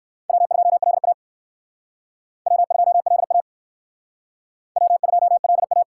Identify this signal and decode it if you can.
Morse code, heard as a keyed tone, about 58 wpm, 700 Hz. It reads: W2LA W2LA W2LA